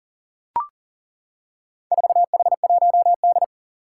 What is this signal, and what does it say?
Morse code, heard as a keyed tone, about 40 wpm, 700 Hz. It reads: 4H1D
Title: Four Challenging Letter-Number Combinations 40wpm